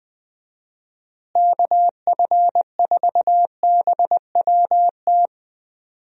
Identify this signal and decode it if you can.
Morse code, heard as a keyed tone, about 20 wpm, 700 Hz. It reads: KF4BWT
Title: USA Call Signs - 6 Characters - Repeated 3x - 20wpm